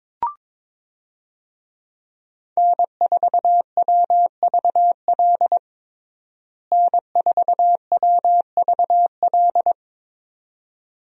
Morse code, heard as a keyed tone, about 22 wpm, 700 Hz. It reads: N4WVL N4WVL
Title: USA Call Signs - 5 Characters - Repeated 2x - 22wpm